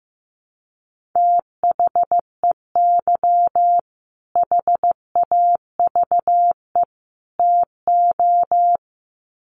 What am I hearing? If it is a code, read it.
Morse code, heard as a keyed tone, about 15 wpm, 700 Hz. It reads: THEY HAVE TO
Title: Sets of 3 Words 15wpm